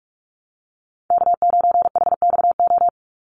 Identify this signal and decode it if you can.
Morse code, heard as a keyed tone, about 44 wpm, 700 Hz. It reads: K9HXO
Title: Call Signs Repeated 2x - 45wpm